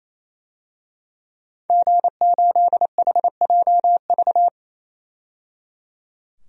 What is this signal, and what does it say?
Morse code, heard as a keyed tone, about 28 wpm, 700 Hz. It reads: G8HJV
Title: DXCC-100 Call Signs - 5 Characters - 28wpm